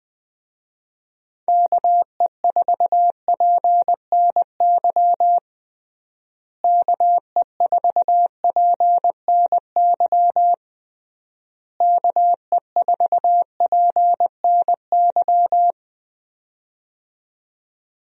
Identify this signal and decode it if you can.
Morse code, heard as a keyed tone, about 20 wpm, 700 Hz. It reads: KE4PNY KE4PNY KE4PNY
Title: USA Call Signs - 6 Characters - Repeated 3x - 20wpm